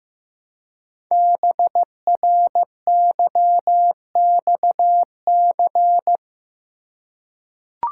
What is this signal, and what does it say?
Morse code, heard as a keyed tone, about 15 wpm, 700 Hz. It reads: BRYXC
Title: Code Groups - 5 Letters - 15wpm